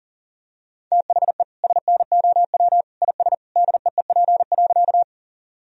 Morse code, heard as a keyed tone, about 40 wpm, 700 Hz. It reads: THE SNOW IS DEEP.
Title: Sentences from Top 1000 Words - Review Part 3 - 40wpm